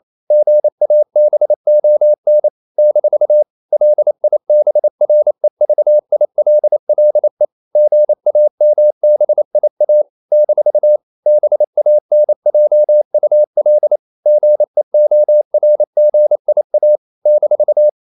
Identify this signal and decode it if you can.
Morse code, heard as a keyed tone, about 28 wpm, 600 Hz. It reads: GABON - LIBREVILLE GAMBIA - BANJUL GEORGIA -
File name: CW / Practice / NationalCapitals28